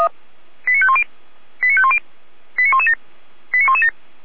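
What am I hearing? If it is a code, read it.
Five-tone selective call (ZVEI), heard as quick runs of five tones, tones of about 70 ms. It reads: 89510 89510 89117 89117
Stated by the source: Received by SDR, ZVEI five-tone selective call.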